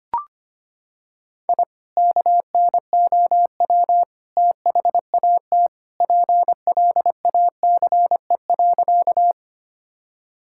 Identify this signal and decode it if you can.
Morse code, heard as a keyed tone, about 25 wpm, 700 Hz. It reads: I KNOW THAT PLACE.
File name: practice / 25wpm / Sentences with 2-Word Phrases - 25wpm